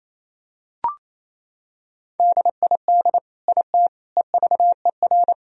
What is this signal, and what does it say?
Morse code, heard as a keyed tone, about 28 wpm, 700 Hz. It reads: DID IT EVER
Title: Sentences from Top 1000 Words - Review Part 2 - 28wpm